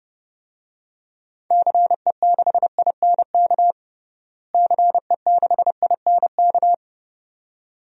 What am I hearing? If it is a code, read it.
Morse code, heard as a keyed tone, about 30 wpm, 700 Hz. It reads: CE6INK CE6INK
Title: DXCC-100 Call Signs - 6 Characters - Repeated 2x - 30wpm